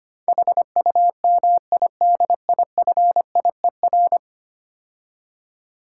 Morse code, heard as a keyed tone, about 27 wpm, 700 Hz. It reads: HUMIDIFIER
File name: practice / 25wpm / Difficult Words - Speed-Racing with Hints - 25wpm